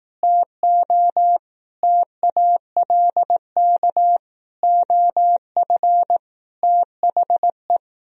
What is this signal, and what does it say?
Morse code, heard as a keyed tone, about 18 wpm, 700 Hz. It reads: TO TALK OF THE
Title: Speed-Racing - Sets of 5 Words 15wpm